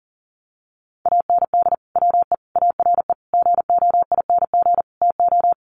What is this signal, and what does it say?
Morse code, heard as a keyed tone, about 40 wpm, 700 Hz. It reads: AND WE ARE GOING TO
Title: Sets of 5 Words Encore2 40wpm